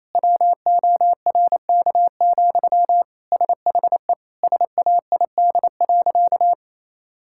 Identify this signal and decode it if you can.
Morse code, heard as a keyed tone, about 27 wpm, 700 Hz. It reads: WORK, SHE SAID.
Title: Sentences from Top 1000 Words - Review Part 6 - 28wpm